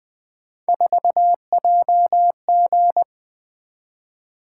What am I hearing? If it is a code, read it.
Morse code, heard as a keyed tone, about 20 wpm, 700 Hz. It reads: 4JG
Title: Three Letter-Number Combinations - 20wpm